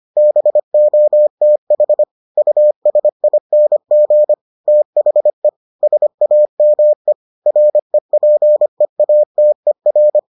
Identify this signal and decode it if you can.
Morse code, heard as a keyed tone, about 25 wpm, 600 Hz. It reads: BOTH USING THE SAME REPEATER